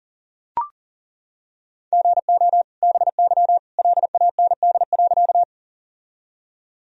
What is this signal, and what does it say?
Morse code, heard as a keyed tone, about 37 wpm, 700 Hz. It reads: GO BY LAND.
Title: Sentences from Top 1000 Words - Review Part 1 - 40wpm